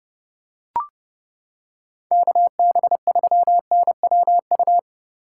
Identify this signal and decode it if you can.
Morse code, heard as a keyed tone, about 29 wpm, 700 Hz. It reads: KB3NWU